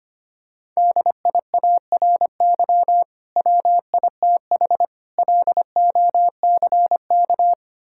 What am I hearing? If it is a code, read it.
Morse code, heard as a keyed tone, about 25 wpm, 700 Hz. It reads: DIARY WITH LOCK
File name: practice / 25wpm / Difficult Words - Speed-Racing with Hints - 25wpm